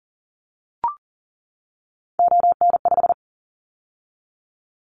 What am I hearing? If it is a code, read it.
Morse code, heard as a keyed tone, about 38 wpm, 700 Hz. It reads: ON5
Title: Callsign Prefixes for Top 50 DXCC - 40wpm